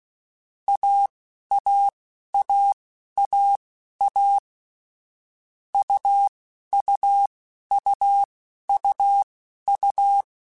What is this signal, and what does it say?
Morse code, heard as a keyed tone, about 16 wpm, spaced as for 11 wpm, 800 Hz. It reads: AAAAA UUUUU
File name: Difficult Combos / Difficult Combos-mp3s / AUV AUV4 @10